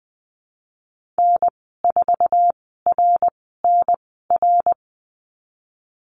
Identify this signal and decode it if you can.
Morse code, heard as a keyed tone, about 20 wpm, spaced as for 15 wpm, 700 Hz. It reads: N4RNR
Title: 062 Lesson 11g - Letter h - Call Signs 20wpm